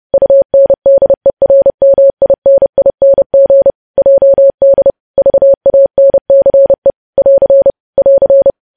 Morse code, heard as a keyed tone, about 30 wpm, 550 Hz. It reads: UNDERMINING JD VANCE <AR> <AR>
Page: news headlines in morse code at 30 wpm